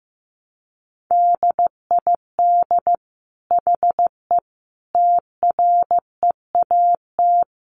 Morse code, heard as a keyed tone, about 15 wpm, 700 Hz. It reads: DID HE TREAT
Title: Sentences from Top 1300 Words 15wpm